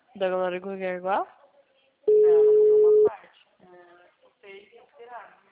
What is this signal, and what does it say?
Ringback tone (Europe, 425 Hz), with speech.
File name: out- -2012- - - .8